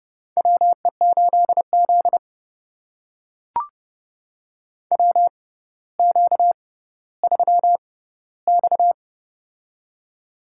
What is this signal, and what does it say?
Morse code, heard as a keyed tone, about 30 wpm, spaced as for 15 wpm, 700 Hz. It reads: WE8Z WQ3X
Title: USA Call Signs - 4 Characters - 6x Character Spacing - 30wpm